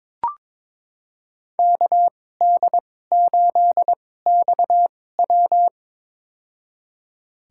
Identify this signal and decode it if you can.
Morse code, heard as a keyed tone, about 22 wpm, spaced as for 16 wpm, 700 Hz. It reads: KD8XW
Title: USA Call Signs - 5 Characters - 2x Character Spacing - 22wpm